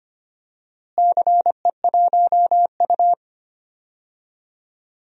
Morse code, heard as a keyed tone, about 25 wpm, 700 Hz. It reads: CE1U
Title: DXCC-100 Call Signs - 4 Characters - Repeated 3x - 25wpm